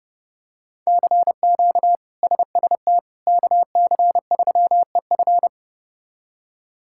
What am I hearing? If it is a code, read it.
Morse code, heard as a keyed tone, about 30 wpm, 700 Hz. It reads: CQ SST KC3EF
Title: K1USN SST QSO Practice - 30wpm